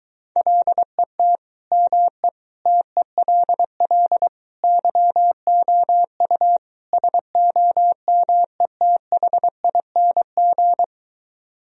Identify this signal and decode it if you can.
Morse code, heard as a keyed tone, about 23 wpm, 690 Hz. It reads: LET ME TELL YOU SOMETHING